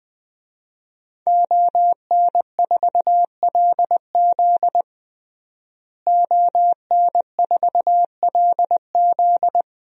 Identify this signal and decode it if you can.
Morse code, heard as a keyed tone, about 21 wpm, 700 Hz. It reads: ON4LZ ON4LZ
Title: DXCC-100 Call Signs - 5 Characters - Repeated 3x - 20wpm